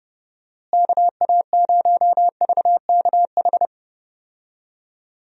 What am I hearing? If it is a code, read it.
Morse code, heard as a keyed tone, about 30 wpm, 700 Hz. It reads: KA0VKH
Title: USA Call Signs - 6 Characters - Repeated 3x - 30wpm